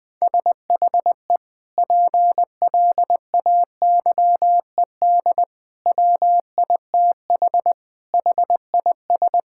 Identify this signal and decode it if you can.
Morse code, heard as a keyed tone, about 21 wpm, 700 Hz. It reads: SHE PLAYED WITH HIS
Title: Sentences with 2-Word Phrases - 20wpm